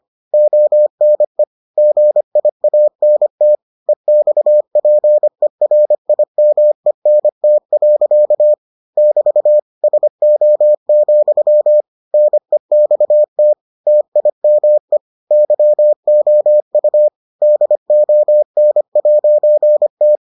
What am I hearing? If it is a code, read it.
Morse code, heard as a keyed tone, about 25 wpm, 600 Hz. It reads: ONE GIANT EXPERIMENT. <BT> SO, NEXT TIME YOU DON'T